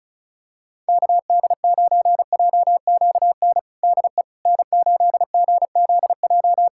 Morse code, heard as a keyed tone, about 34 wpm, 700 Hz. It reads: KD9JQN DE N8GZJ
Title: K1USN SST QSO Practice 35wpm